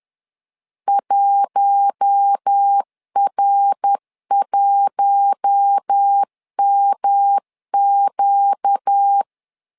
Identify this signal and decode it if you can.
Morse code, heard as a keyed tone, about 10 wpm, 800 Hz. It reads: 1R1MQ
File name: LEZIONI / 4 Q1 10wpm 2448